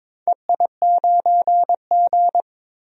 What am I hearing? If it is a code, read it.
Morse code, heard as a keyed tone, about 22 wpm, 700 Hz. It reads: EI9G
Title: DXCC-100 Call Signs - 4 Characters - Repeated 2x - 22wpm